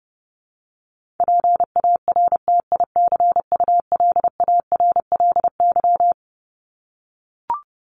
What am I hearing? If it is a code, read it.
Morse code, heard as a keyed tone, about 30 wpm, 700 Hz. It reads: PARTICULARLY
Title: LY Suffix - Top 100 Words with Base Words 30wpm